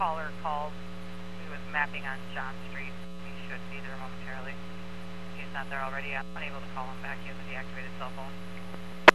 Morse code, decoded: none

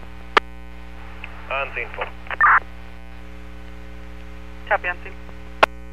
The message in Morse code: none

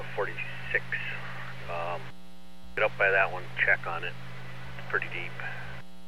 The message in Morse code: none

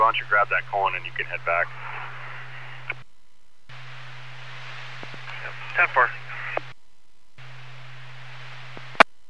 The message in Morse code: none

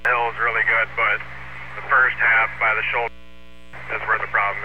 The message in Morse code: none

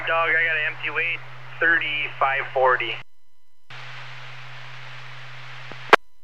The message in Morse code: none

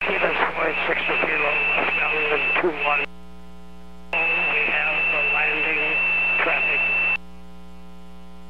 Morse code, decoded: none

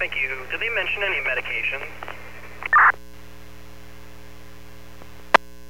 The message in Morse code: none

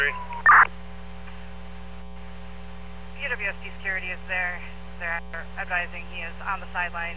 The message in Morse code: none

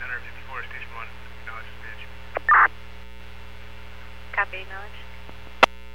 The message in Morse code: none

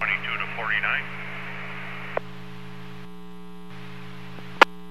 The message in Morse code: none

